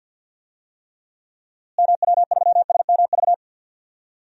Morse code, heard as a keyed tone, about 49 wpm, 700 Hz. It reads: MW3SMV
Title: DXCC-100 Call Signs - 6 Characters - Repeated 2x - 50wpm